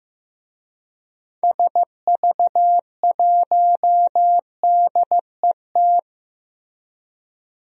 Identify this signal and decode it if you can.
Morse code, heard as a keyed tone, about 15 wpm, 700 Hz. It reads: SV1DET
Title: DXCC-100 Call Signs - 6 Characters - Repeated 3x - 15wpm